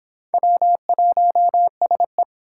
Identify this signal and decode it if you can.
Morse code, heard as a keyed tone, about 25 wpm, 700 Hz. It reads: W1SE